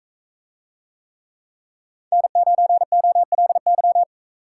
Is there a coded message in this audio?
Yes: Morse code from 2.1 to 4.0 s, heard as a keyed tone, about 42 wpm, 700 Hz. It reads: N9OLY